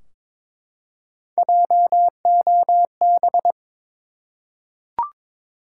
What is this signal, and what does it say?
Morse code, heard as a keyed tone, about 22 wpm, 700 Hz. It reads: JOB
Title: Three Letter Words - 22wpm